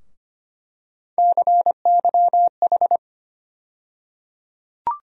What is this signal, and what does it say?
Morse code, heard as a keyed tone, about 25 wpm, 700 Hz. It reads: CYH
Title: USA Call Sign Postfixes - 3 Letters 25wpm